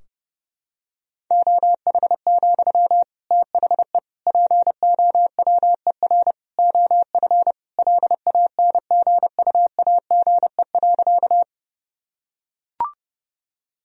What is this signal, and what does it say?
Morse code, heard as a keyed tone, about 30 wpm, 700 Hz. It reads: OH, THE POWER OF LANGUAGE.
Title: Sentences from Top 1000 Words - Review Part 3 - 30wpm